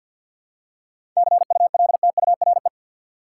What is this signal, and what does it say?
Morse code, heard as a keyed tone, about 53 wpm, 700 Hz. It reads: CULTURE
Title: Seven Letter Words - 50wpm